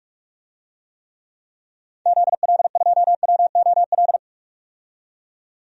Morse code, heard as a keyed tone, about 42 wpm, 700 Hz. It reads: ZL2WOL
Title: DXCC-100 Call Signs - 6 Characters - 45wpm